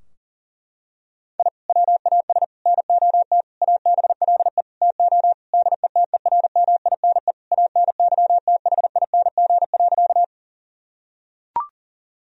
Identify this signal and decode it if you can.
Morse code, heard as a keyed tone, about 40 wpm, 700 Hz. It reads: I WAS NOT ABLE TO DETERMINE ANYTHING.
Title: Sentences from Top 1000 Words - Review Part 4 - 40wpm